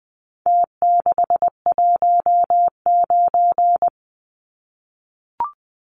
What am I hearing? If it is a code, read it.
Morse code, heard as a keyed tone, about 20 wpm, 700 Hz. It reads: T619
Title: Numbers - Contest Style - 4-Digit 20wpm